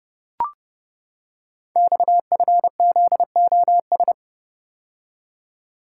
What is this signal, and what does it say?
Morse code, heard as a keyed tone, about 30 wpm, 700 Hz. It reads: XFZOS
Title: Code Groups - 5 Letters - 30wpm